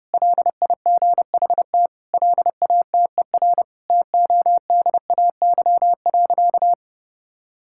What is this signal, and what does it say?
Morse code, heard as a keyed tone, about 30 wpm, 700 Hz. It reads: LIGHT LATER TODAY.
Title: Sentences from Top 1000 Words - Review Part 6 - 30wpm